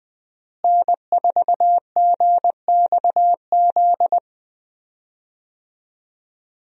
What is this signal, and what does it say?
Morse code, heard as a keyed tone, about 20 wpm, 700 Hz. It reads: N4GXZ
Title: USA Call Signs - 5 Characters - Repeated 3x - 20wpm